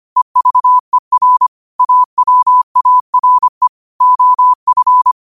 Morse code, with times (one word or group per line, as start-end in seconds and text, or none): 0.2-1.5 EVER
1.8-3.7 AWARE
4.0-5.1 OF